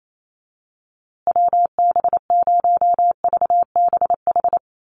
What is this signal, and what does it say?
Morse code, heard as a keyed tone, about 28 wpm, 700 Hz. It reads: WB0VBH
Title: USA Call Signs Repeated 3x 28wpm